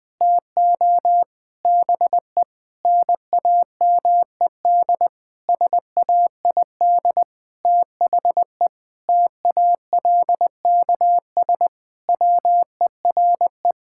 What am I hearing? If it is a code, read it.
Morse code, heard as a keyed tone, about 20 wpm, 700 Hz. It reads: TO BE NAMED SAID THE TALKS WERE
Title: Sentences with 8-Word Phrases - 20wpm